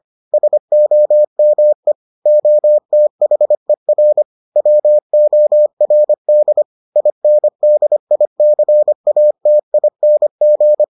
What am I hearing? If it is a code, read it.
Morse code, heard as a keyed tone, about 25 wpm, 600 Hz. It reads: SOME OTHER WORD INDICATING